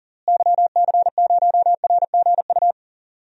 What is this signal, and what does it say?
Morse code, heard as a keyed tone, about 39 wpm, 700 Hz. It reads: YC0RGU
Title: DXCC-100 Call Signs - 6 Characters - Repeated 2x - 40wpm